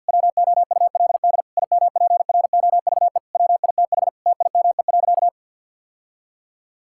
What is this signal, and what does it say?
Morse code, heard as a keyed tone, about 50 wpm, 700 Hz. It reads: WOULD IMPROVE WITH TIME.
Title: Sentences from Top 1000 Words - Review Part 5 - 50wpm